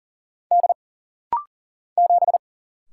Morse code, heard as a keyed tone, about 41 wpm, 700 Hz. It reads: D 7
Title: Single Letter-Number - Rapid-Fire 40wpm